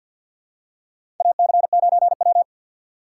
Morse code, heard as a keyed tone, about 52 wpm, 700 Hz. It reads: AC9W
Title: USA Call Signs - 4 Characters - 2x Character Spacing - 50wpm